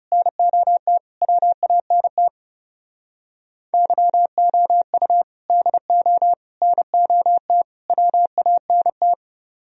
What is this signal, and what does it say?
Morse code, heard as a keyed tone, about 29 wpm, 700 Hz. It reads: NOT WANT YOU DO NOT WANT
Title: Speed-Racing - Sets of 4 Words 30wpm